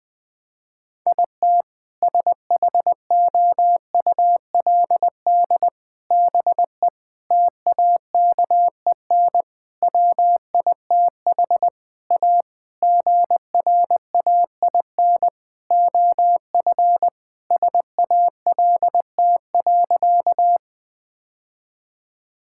Morse code, heard as a keyed tone, about 20 wpm, 700 Hz. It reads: IT SHOULD BE TAKEN WITH A GRAIN OF SALT.
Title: Sentences with 8-Word Phrases - 20wpm